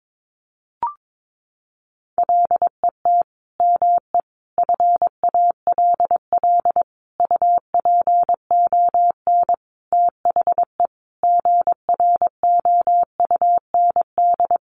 Morse code, heard as a keyed tone, about 22 wpm, 700 Hz. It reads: LET ME FALL UPON THE GROUND
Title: Sentences from Top 1100 Words 22wpm